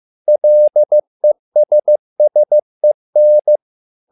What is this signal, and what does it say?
Morse code, heard as a keyed tone, about 15 wpm, 600 Hz. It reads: LESSEN
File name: learning center / tower trouble / tower trouble 15wpm